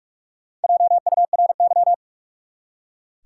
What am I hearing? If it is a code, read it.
Morse code, heard as a keyed tone, about 42 wpm, 700 Hz. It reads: JURY
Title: Four Letter Words - 45wpm